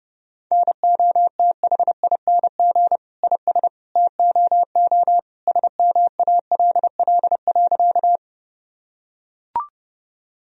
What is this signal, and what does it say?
Morse code, heard as a keyed tone, about 30 wpm, 700 Hz. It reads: NOTHING IS TOO SMALL.
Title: Sentences from Top 1000 Words - Review Part 1 - 30wpm